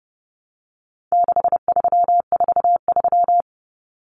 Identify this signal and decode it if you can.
Morse code, heard as a keyed tone, about 30 wpm, 700 Hz. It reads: 6343
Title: Numbers - 4-Digit 30wpm